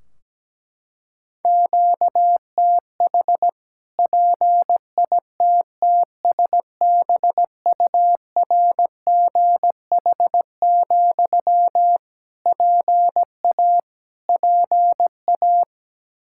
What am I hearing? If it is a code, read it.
Morse code, heard as a keyed tone, about 17 wpm, 700 Hz. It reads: QTH PITTSBURGH, PA PA
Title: Practice QSO - Exchange 1 - Beginning - 17wpm